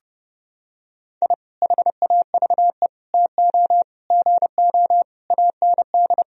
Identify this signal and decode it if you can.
Morse code, heard as a keyed tone, about 30 wpm, 700 Hz. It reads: I HAVE TO GO AND
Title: Sentences from Top 1000 Words - Review Part 4 - 30wpm